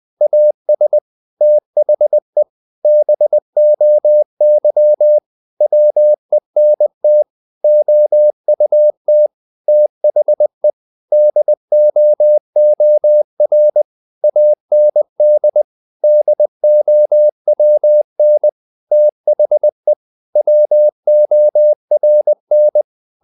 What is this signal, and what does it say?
Morse code, heard as a keyed tone, about 20 wpm, 600 Hz. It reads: AS THE BOY WENT OUT THE DOOR AND DOWN THE WORN
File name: the old man and the sea part 2 20wpm